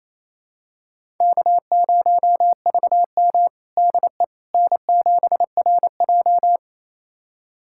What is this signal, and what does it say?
Morse code, heard as a keyed tone, about 28 wpm, 700 Hz. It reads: K0VM DE N7RJ
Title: USA Callsign DE Callsign - 4 Characters - 28wpm